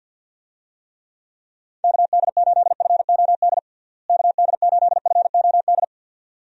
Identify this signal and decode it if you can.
Morse code, heard as a keyed tone, about 50 wpm, 700 Hz. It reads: KD8FOD KD8FOD
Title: USA Call Signs - 6 Characters - Repeated 2x - 50wpm